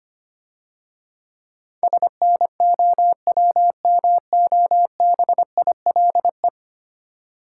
Morse code, heard as a keyed tone, about 25 wpm, 700 Hz. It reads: SNOWMOBILE
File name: practice / 25wpm / Difficult Words - Speed-Racing with Hints - 25wpm